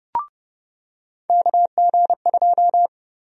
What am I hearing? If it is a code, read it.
Morse code, heard as a keyed tone, about 31 wpm, 700 Hz. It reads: KG2
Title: USA Call Sign Prefixes - 30wpm